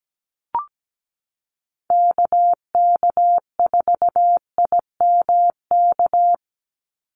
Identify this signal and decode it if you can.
Morse code, heard as a keyed tone, about 17 wpm, 700 Hz. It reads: KK4IMK
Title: USA Call Signs 17wpm